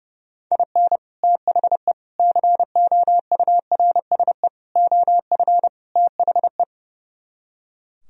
Morse code, heard as a keyed tone, about 38 wpm, 700 Hz. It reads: IN THE COURSE OF THE
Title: Speed-Racing - Sets of 5 Words 30wpm